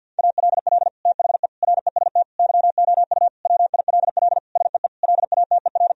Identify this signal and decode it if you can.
Morse code, heard as a keyed tone, about 50 wpm, 700 Hz. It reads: ALL THE REST YOU WILL SEE LATER
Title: Sentences from Top 1000 Words - Review Part 4 - 50wpm